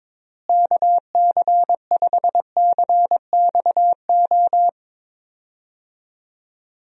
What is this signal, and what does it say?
Morse code, heard as a keyed tone, about 21 wpm, 700 Hz. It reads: KC5CXO
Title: USA Call Signs - 6 Characters - Repeated 2x - 22wpm